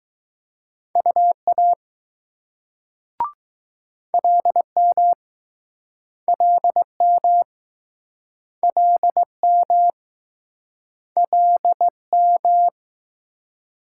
Morse code, heard as a keyed tone, about 20 wpm, 700 Hz. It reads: UA LM LM LM LM